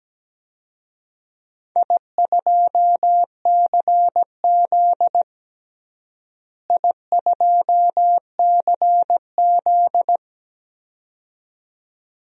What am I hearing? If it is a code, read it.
Morse code, heard as a keyed tone, about 17 wpm, 700 Hz. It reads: I2CZ I2CZ
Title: DXCC-100 Call Signs - 4 Characters - Repeated 2x - 17wpm